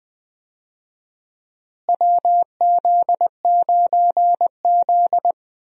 Morse code, heard as a keyed tone, about 20 wpm, 700 Hz. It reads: WZ9Z
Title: USA Call Signs - 4 Characters - Repeated 3x - 20wpm